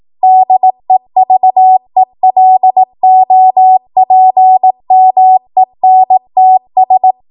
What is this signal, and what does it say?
Morse code, heard as a keyed tone, about 18 wpm, 750 Hz. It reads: DEVELOPMENTS
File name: Morse / 18WPM